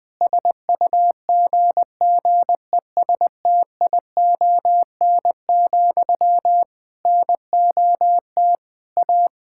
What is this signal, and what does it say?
Morse code, heard as a keyed tone, about 20 wpm, 700 Hz. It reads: SUGGESTION, NOT A
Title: Sentences from Top 1600 Words 20wpm